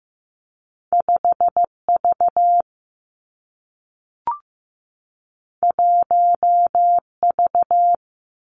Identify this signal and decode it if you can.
Morse code, heard as a keyed tone, about 15 wpm, 700 Hz. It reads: 5 V 1 V
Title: Two Challenging Letter-Number Combinations 15wpm